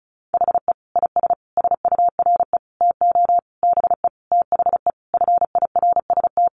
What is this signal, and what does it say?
Morse code, heard as a keyed tone, about 34 wpm, 700 Hz. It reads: HE IS SURE TO BE THE FIRST
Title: Sentences from Top 1400 Words 35wpm